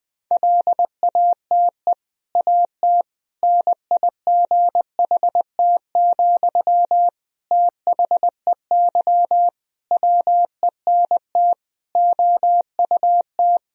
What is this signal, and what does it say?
Morse code, heard as a keyed tone, about 20 wpm, 700 Hz. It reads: LATE AT NIGHT, THEY WENT OUT
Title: Sentences from Top 1000 Words - Review Part 3 - 20wpm